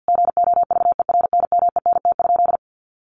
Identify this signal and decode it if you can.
Morse code, heard as a keyed tone, about 50 wpm, 700 Hz. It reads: GOVERNMENT?
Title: Sentences from Top 1400 Words 50wpm